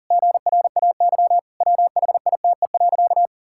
Morse code, heard as a keyed tone, about 40 wpm, 700 Hz. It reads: GRAY WHITE.
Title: Pangram Challenge v2 - 40wpm